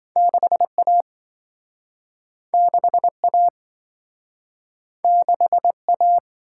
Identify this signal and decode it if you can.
Morse code, heard as a keyed tone, about 27 wpm, 700 Hz. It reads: 6A 6A 6A